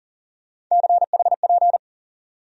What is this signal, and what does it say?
Morse code, heard as a keyed tone, about 40 wpm, 700 Hz. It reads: CHP
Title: USA Call Sign Postfixes - 3 Letters 40wpm